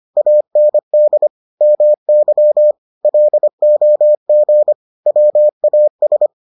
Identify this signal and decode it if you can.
Morse code, heard as a keyed tone, about 25 wpm, 600 Hz. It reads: AND MY LOG WAS